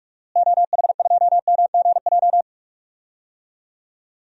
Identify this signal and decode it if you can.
Morse code, heard as a keyed tone, about 46 wpm, 700 Hz. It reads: OH2MGJ
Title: DXCC-100 Call Signs - 6 Characters - Repeated 3x - 45wpm